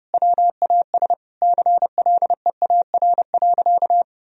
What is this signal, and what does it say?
Morse code, heard as a keyed tone, about 30 wpm, 700 Hz. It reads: WAS CLEAR.
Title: Sentences from Top 1000 Words - Review Part 1 - 30wpm